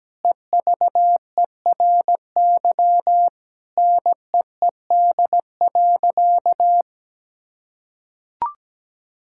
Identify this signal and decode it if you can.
Morse code, heard as a keyed tone, about 17 wpm, 700 Hz. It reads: EVERY NEED.
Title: Sentences from Top 1000 Words - Review Part 5 - 17wpm